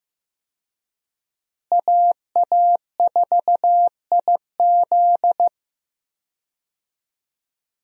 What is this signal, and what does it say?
Morse code, heard as a keyed tone, about 16 wpm, 700 Hz. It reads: AA4IZ
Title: USA Call Signs - 5 Characters - Repeated 3x - 15wpm